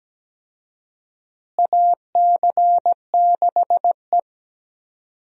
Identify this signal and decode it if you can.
Morse code, heard as a keyed tone, about 17 wpm, 700 Hz. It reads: AC6E
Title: USA Call Signs - 4 Characters - Repeated 2x - 17wpm